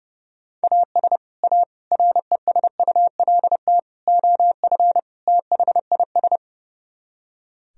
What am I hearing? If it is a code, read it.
Morse code, heard as a keyed tone, about 34 wpm, 700 Hz. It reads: AS A RESULT OF THIS
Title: Speed-Racing - Sets of 5 Words 30wpm